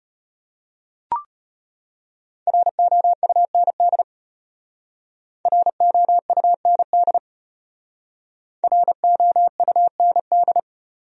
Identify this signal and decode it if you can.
Morse code, heard as a keyed tone, about 32 wpm, 700 Hz. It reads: ROUND ROUND ROUND